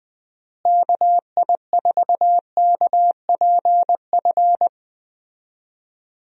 Morse code, heard as a keyed tone, about 20 wpm, 700 Hz. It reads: KI4KPF
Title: USA Call Signs - 6 Characters - Repeated 3x - 20wpm